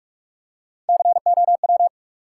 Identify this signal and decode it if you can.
Morse code, heard as a keyed tone, about 48 wpm, 700 Hz. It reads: COW
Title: Three Letter Words - 45wpm